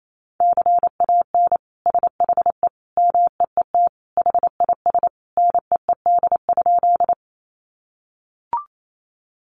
Morse code, heard as a keyed tone, about 28 wpm, 700 Hz. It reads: CAN SHE MEET HIS NEED?
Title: Sentences from Top 1100 Words 28wpm